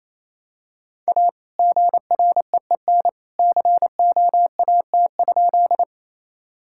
Morse code, heard as a keyed tone, about 28 wpm, 700 Hz. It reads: A GREEN COAT?
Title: Sentences from Top 1600 Words 28wpm